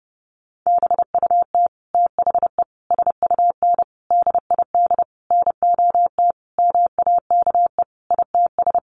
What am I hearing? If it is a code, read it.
Morse code, heard as a keyed tone, about 30 wpm, 700 Hz. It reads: BUT THE SUN DID NOT MAKE ITS
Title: Sentences from Top 1100 Words 30wpm